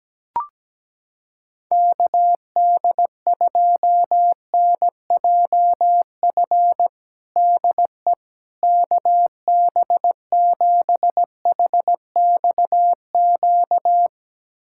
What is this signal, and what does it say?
Morse code, heard as a keyed tone, about 17 wpm, 700 Hz. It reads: KD2NJF DE KB7HXQ
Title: USA Callsign DE Callsign - 6 Characters - 17wpm